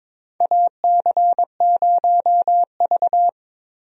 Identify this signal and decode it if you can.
Morse code, heard as a keyed tone, about 22 wpm, 700 Hz. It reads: AC0V
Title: USA Call Signs - 4 Characters - Repeated 3x - 22wpm